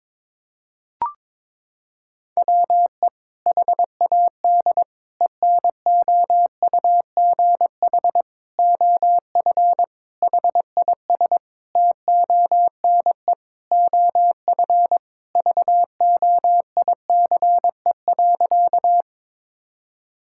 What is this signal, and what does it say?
Morse code, heard as a keyed tone, about 22 wpm, 700 Hz. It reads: WE HAD ENOUGH OF HIS TONE OF VOICE.
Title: Sentences from Top 1600 Words 22wpm